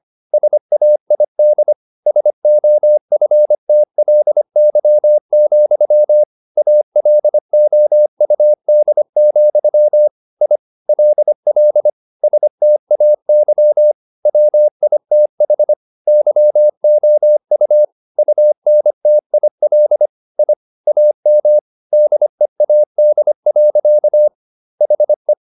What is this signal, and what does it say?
Morse code, heard as a keyed tone, about 25 wpm, 600 Hz. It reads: SAID SOFTLY, ALOUD, I LL STAY WITH YOU UNTIL I AM DEAD. HE
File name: the old man and the sea part 1 25wpm